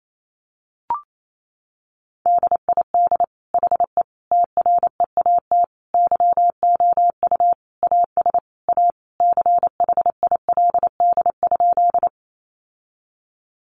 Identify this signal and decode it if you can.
Morse code, heard as a keyed tone, about 28 wpm, 700 Hz. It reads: DID HE TREAT YOU AS A CHILD?
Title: Sentences from Top 1300 Words 28wpm